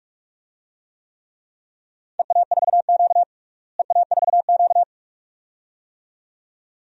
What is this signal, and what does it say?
Morse code, heard as a keyed tone, about 46 wpm, 700 Hz. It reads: EA4Q EA4Q
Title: DXCC-100 Call Signs - 4 Characters - Repeated 2x - 45wpm